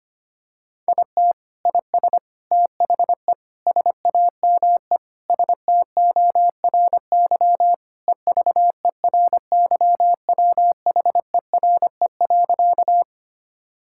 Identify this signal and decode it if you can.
Morse code, heard as a keyed tone, about 25 wpm, 700 Hz. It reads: IT IS THE SAME STORY EVERYWHERE.
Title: Sentences from Top 1600 Words 25wpm